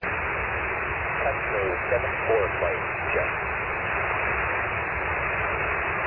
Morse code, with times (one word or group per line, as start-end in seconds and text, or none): none